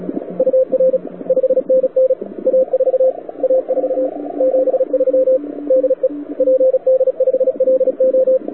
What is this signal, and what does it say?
Morse code, heard as a keyed tone, about 35 wpm, 520 Hz, with radio noise. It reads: UR 5NN A4 A4 73 DE PD5LO
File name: wwcw08 / k5zd wwcw08 1400z